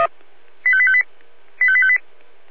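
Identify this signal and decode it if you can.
Five-tone selective call (ZVEI), heard as quick runs of five tones, tones of about 70 ms. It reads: 85858 85858
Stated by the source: Five-tone ZVEI selective call received on SDR, code 85320.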